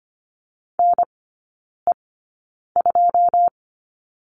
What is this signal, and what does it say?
Morse code, heard as a keyed tone, about 25 wpm, spaced as for 9 wpm, 700 Hz. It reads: NE2
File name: Lesson 14e - Number 2 - Three Characters 25wpm